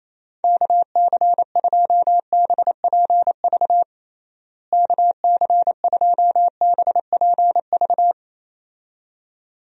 Morse code, heard as a keyed tone, about 28 wpm, 700 Hz. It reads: KC2BPV KC2BPV
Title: USA Call Signs - 6 Characters - Repeated 2x - 28wpm